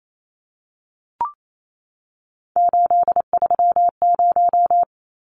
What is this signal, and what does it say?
Morse code, heard as a keyed tone, about 28 wpm, 700 Hz. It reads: 830
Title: Numbers - 3-Digit 28wpm